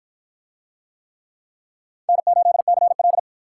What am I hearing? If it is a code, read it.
Morse code, heard as a keyed tone, about 47 wpm, 700 Hz. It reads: N8CL